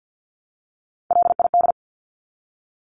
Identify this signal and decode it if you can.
Morse code, heard as a keyed tone, about 50 wpm, 700 Hz. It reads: LID
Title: Abbreviations 50wpm